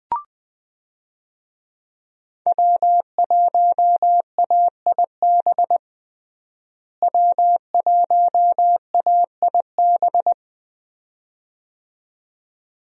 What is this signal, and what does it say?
Morse code, heard as a keyed tone, about 20 wpm, 700 Hz. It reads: W1AIB W1AIB
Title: USA Call Signs - 5 Characters - Repeated 2x - 20wpm